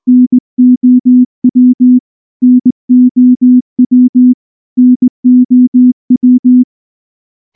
Morse code, heard as a keyed tone, about 19 wpm, 250 Hz. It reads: NOW NOW NOW
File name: morse-training-20WPM